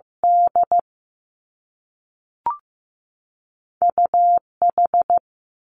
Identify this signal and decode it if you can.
Morse code, heard as a keyed tone, about 15 wpm, 700 Hz. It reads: D U H
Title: Two Challenging Letter-Number Combinations 15wpm